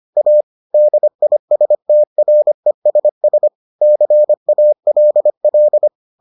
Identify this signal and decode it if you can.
Morse code, heard as a keyed tone, about 25 wpm, 600 Hz. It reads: A DISTRESS CALL